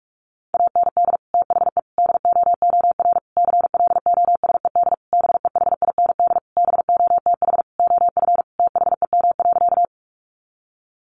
Morse code, heard as a keyed tone, about 44 wpm, 700 Hz. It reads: AND THE DOOR CLOSED BEHIND BOTH OF THEM.
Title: Sentences from Top 1400 Words 45wpm